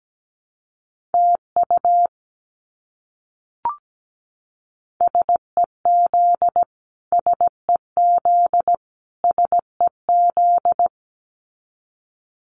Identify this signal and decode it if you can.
Morse code, heard as a keyed tone, about 17 wpm, 700 Hz. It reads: TU SEZ SEZ SEZ
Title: Abbreviations Repeated 3x - 17wpm